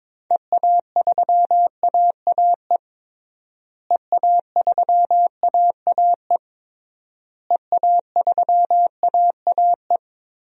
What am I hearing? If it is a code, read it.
Morse code, heard as a keyed tone, about 22 wpm, 700 Hz. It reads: EA3AAE EA3AAE EA3AAE
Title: DXCC-100 Call Signs - 6 Characters - Repeated 3x - 22wpm